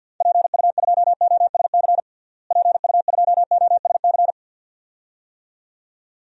Morse code, heard as a keyed tone, about 52 wpm, 700 Hz. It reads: PU2OSC PU2OSC
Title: DXCC-100 Call Signs - 6 Characters - Repeated 2x - 50wpm